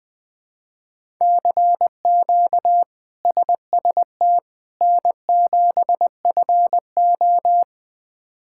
Morse code, heard as a keyed tone, about 20 wpm, 700 Hz. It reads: CQ SST N7FO
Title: K1USN SST QSO Practice - 20wpm